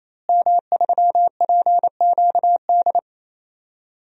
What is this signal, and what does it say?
Morse code, heard as a keyed tone, about 28 wpm, 700 Hz. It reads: M3PQD
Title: DXCC-100 Call Signs - 5 Characters - Repeated 3x - 28wpm